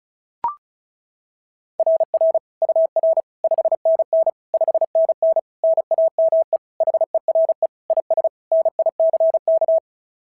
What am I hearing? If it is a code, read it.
Morse code, heard as a keyed tone, about 36 wpm, 690 Hz. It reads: RR UR 5NN 5NN NAME HERE IS NICK
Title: SOTA QSO Practice 35wpm